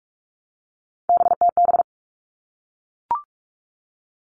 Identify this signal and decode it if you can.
Morse code, heard as a keyed tone, about 44 wpm, 700 Hz. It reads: BTB
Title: All Cut Numbers - 3-Digit 45wpm